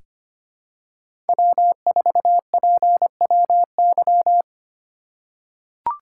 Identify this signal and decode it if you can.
Morse code, heard as a keyed tone, about 25 wpm, 700 Hz. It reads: W4PWY
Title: USA Call Signs - 5 Characters - 3x Character Spacing - 25wpm